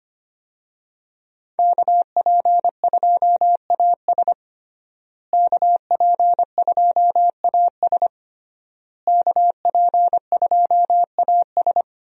Morse code, heard as a keyed tone, about 25 wpm, 700 Hz. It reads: KP2AS KP2AS KP2AS
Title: USA Call Signs - 5 Characters - Repeated 3x - 25wpm